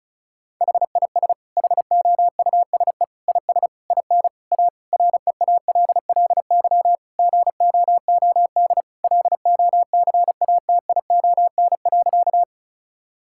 Morse code, heard as a keyed tone, about 36 wpm, 690 Hz. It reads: HIS HOUSE IS IN A REALLY GOOD LOCATION.
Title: Sentences from Top 1000 Words - Review Part 6 - 35wpm